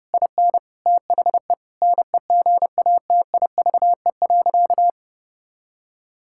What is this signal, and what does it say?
Morse code, heard as a keyed tone, about 32 wpm, 700 Hz. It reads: IN THE NEGATIVE.
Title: Sentences from Top 1000 Words - Review Part 6 - 30wpm